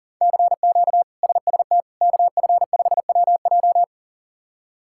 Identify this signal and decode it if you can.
Morse code, heard as a keyed tone, about 40 wpm, 700 Hz. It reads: CQ SST KF5WJ
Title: K1USN SST QSO Practice - 40wpm